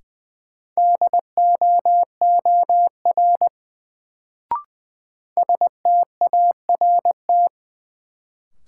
Morse code, heard as a keyed tone, about 20 wpm, 700 Hz. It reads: DOOR START
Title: Sentences from Top 300 Words - New Word Drill - 20wpm